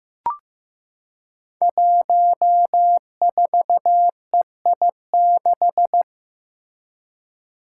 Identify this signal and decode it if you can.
Morse code, heard as a keyed tone, about 15 wpm, 700 Hz. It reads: 14EI6
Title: Five Challenging Letter-Number Combinations 15wpm